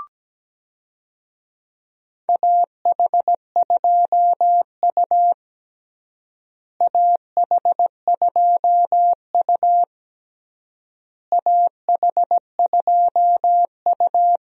Morse code, heard as a keyed tone, about 17 wpm, 700 Hz. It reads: AH2U AH2U AH2U
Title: USA Call Signs - 4 Characters - Repeated 3x - 17wpm